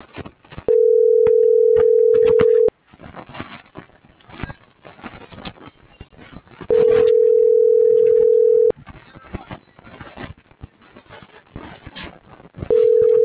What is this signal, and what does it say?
Ringback tone (North America), 440 + 480 Hz.
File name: CENTRAL TERRESTRE / out claudiacruz - .62178